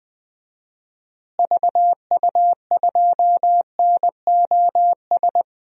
Morse code, heard as a keyed tone, about 20 wpm, 700 Hz. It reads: VU2NOS
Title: DXCC-100 Call Signs - 6 Characters - Repeated 2x - 20wpm